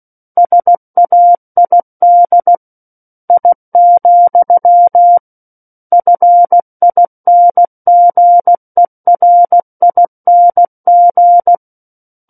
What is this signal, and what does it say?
Morse code, heard as a keyed tone, about 16 wpm, 700 Hz. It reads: SAID I, FINGERING
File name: Book One / 07-Chapter 7 - 16 WPM